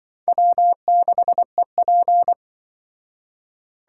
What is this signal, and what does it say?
Morse code, heard as a keyed tone, about 25 wpm, 700 Hz. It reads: W6EP
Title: Speed Racing - USA 1x2 Call Signs 20wpm